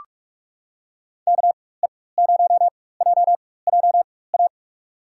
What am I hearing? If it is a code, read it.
Morse code, heard as a keyed tone, about 45 wpm, spaced as for 21 wpm, 700 Hz. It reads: KE0JJA
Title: USA Call Signs - 6 Characters - 4x Character Spacing - 45wpm